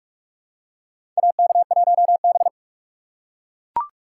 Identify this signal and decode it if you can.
Morse code, heard as a keyed tone, about 43 wpm, 700 Hz. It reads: AK1B
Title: USA Call Signs - 4 Characters - 3x Character Spacing - 45wpm